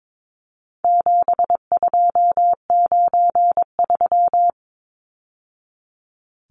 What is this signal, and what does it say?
Morse code, heard as a keyed tone, about 22 wpm, 700 Hz. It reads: 7293
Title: Numbers - 4-Digit 22wpm